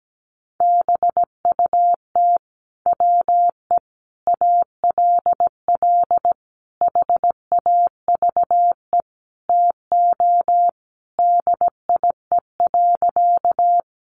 Morse code, heard as a keyed tone, about 17 wpm, 700 Hz. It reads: BUT WE ALL HAVE TO DIE.
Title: Sentences from Top 1300 Words 17wpm